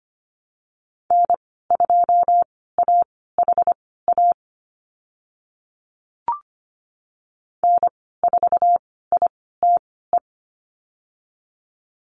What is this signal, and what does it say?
Morse code, heard as a keyed tone, about 25 wpm, spaced as for 16 wpm, 700 Hz. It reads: N2AHA N4ITE
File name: Lesson 15g - Number 5 - Call Signs 25wpm